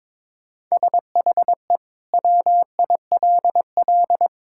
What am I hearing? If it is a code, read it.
Morse code, heard as a keyed tone, about 21 wpm, 700 Hz. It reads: SHE WILL
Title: Sentences from Top 1000 Words - Review Part 6 - 22wpm